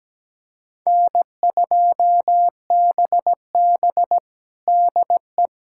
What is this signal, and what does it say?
Morse code, heard as a keyed tone, about 17 wpm, 700 Hz. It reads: N2BB DE
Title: USA Callsign DE Callsign - 4 Characters - 17wpm